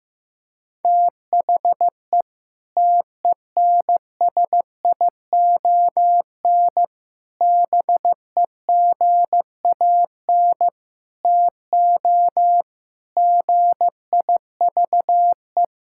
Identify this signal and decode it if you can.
Morse code, heard as a keyed tone, about 16 wpm, 700 Hz. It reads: THE TENSION BEGAN TO GIVE
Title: Sentences from Top 1600 Words 15wpm